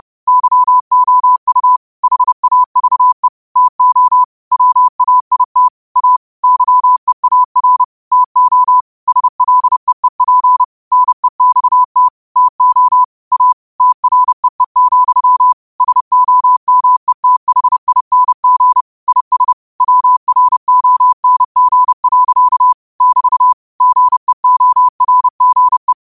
Morse code, heard as a keyed tone, about 30 wpm, 1000 Hz. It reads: YOU HAVE TO WAIT A YEAR TO SLEEP NEXT TO A TREE, SOMETHING IS WRONG. <BT> GEORGE